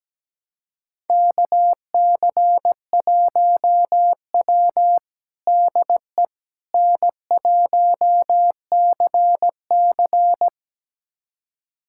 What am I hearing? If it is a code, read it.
Morse code, heard as a keyed tone, about 17 wpm, 700 Hz. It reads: KC1W DE N1CC
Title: USA Callsign DE Callsign - 4 Characters - 17wpm